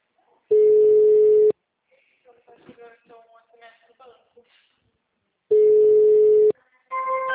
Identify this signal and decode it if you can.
Ringback tone (Europe, 425 Hz), with speech.